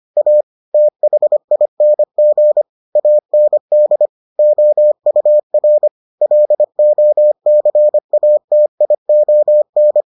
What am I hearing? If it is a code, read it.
Morse code, heard as a keyed tone, about 25 wpm, 600 Hz. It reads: A THING AND OUR LOCATION